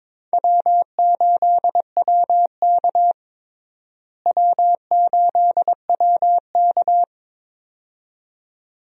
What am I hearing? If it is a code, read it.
Morse code, heard as a keyed tone, about 22 wpm, 700 Hz. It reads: W8WK W8WK
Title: USA Call Signs - 4 Characters - Repeated 2x - 22wpm